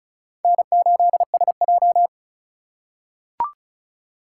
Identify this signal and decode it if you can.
Morse code, heard as a keyed tone, about 36 wpm, 700 Hz. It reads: N8SJ
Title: USA Call Signs - 4 Characters - 4x Character Spacing - 35wpm